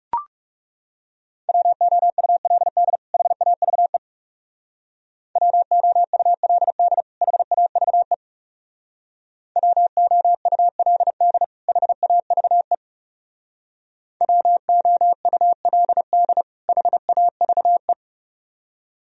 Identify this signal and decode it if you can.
Morse code, heard as a keyed tone, about 35 wpm, 700 Hz. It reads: WOULD HAVE WOULD HAVE WOULD HAVE WOULD HAVE
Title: Speed-Racing - Sets of 2 Words 30wpm